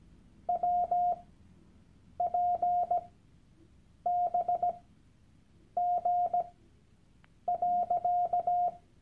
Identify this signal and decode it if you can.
Morse code, heard as a keyed tone, about 17 wpm, spaced as for 7 wpm, 710 Hz. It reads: WPBG.
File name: Workshop Dateien / CW-Online-Kurs / Lektion 001